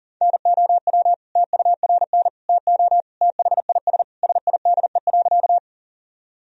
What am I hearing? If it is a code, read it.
Morse code, heard as a keyed tone, about 40 wpm, 700 Hz. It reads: NOW TURN TO THIS SIDE.
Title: Sentences from Top 1000 Words - Review Part 1 - 40wpm